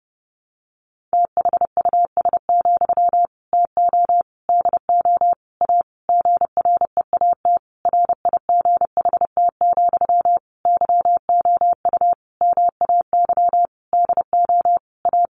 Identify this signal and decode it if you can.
Morse code, heard as a keyed tone, about 30 wpm, 700 Hz. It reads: THUS, TO DO A GREAT RIGHT, YOU MAY DO A
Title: Sentences from Top 900 Words 30wpm